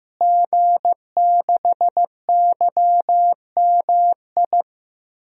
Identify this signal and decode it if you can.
Morse code, heard as a keyed tone, about 15 wpm, 700 Hz. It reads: G6YMI
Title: DXCC-100 Call Signs - 5 Characters - Repeated 2x - 15wpm